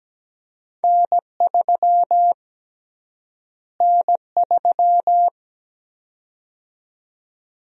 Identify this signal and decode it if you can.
Morse code, heard as a keyed tone, about 17 wpm, 700 Hz. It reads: N3 N3
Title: USA Call Sign Prefixes - Repeated 2x - 17wpm